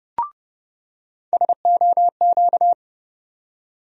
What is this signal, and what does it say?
Morse code, heard as a keyed tone, about 29 wpm, 700 Hz. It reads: SOQ
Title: USA Call Sign Postfixes - 3 Letters 30wpm